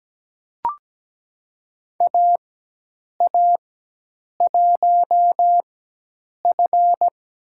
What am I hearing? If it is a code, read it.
Morse code, heard as a keyed tone, about 17 wpm, spaced as for 8 wpm, 700 Hz. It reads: AA1F
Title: USA Call Signs - 4x Character Spacing - 17wpm